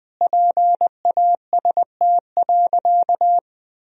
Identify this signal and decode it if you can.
Morse code, heard as a keyed tone, about 20 wpm, 700 Hz. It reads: PAST.
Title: Sentences from Top 1000 Words - Review Part 3 - 20wpm